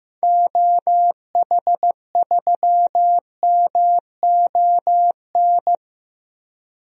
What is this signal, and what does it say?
Morse code, heard as a keyed tone, about 15 wpm, 700 Hz. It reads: OH3MON
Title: DXCC-100 Call Signs - 6 Characters - 15wpm